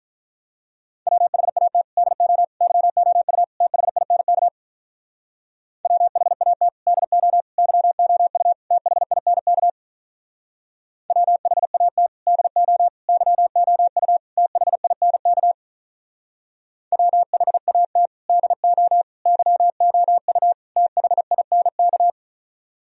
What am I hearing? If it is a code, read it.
Morse code, heard as a keyed tone, about 46 wpm, 700 Hz. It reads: WHAT DO YOU THINK WHAT DO YOU THINK WHAT DO YOU THINK WHAT DO YOU THINK